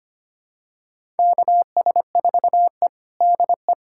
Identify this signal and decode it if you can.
Morse code, heard as a keyed tone, about 25 wpm, 700 Hz. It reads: KS4E DE
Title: USA Callsign DE Callsign - 4 Characters - 25wpm